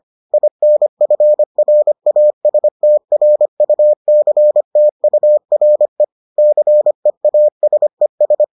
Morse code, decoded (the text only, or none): INFRASTRUCTURE CEASES